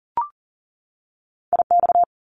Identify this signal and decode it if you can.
Morse code, heard as a keyed tone, about 39 wpm, 700 Hz. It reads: IX
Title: Top Two Letter Combinations in Words 40wpm